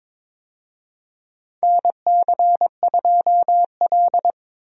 Morse code, heard as a keyed tone, about 22 wpm, 700 Hz. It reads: NC2L
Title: USA Call Signs - 4 Characters - 22wpm